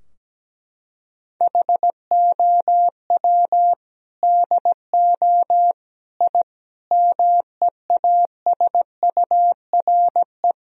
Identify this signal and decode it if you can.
Morse code, heard as a keyed tone, about 17 wpm, 700 Hz. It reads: HOW DO I MEASURE
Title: Sentences from Top 1000 Words - Review Part 2 - 17wpm